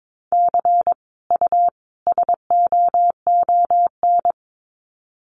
Morse code, heard as a keyed tone, about 22 wpm, 700 Hz. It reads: C U SOON
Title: Common QSO Phrases 22wpm